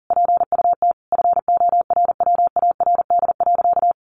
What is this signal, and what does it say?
Morse code, heard as a keyed tone, about 37 wpm, 700 Hz. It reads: PUT FORWARD.
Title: Sentences from Top 1300 Words 40wpm